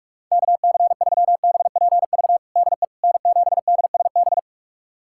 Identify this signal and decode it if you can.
Morse code, heard as a keyed tone, about 46 wpm, 700 Hz. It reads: KC3BPV DE N7DSB
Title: USA Callsign DE Callsign 45wpm